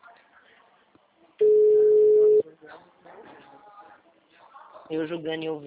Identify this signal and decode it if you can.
Ringback tone (Europe, 425 Hz), with speech.